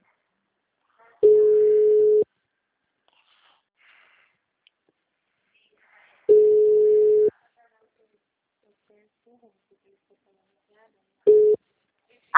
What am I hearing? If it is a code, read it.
Ringback tone (Europe, 425 Hz), with speech.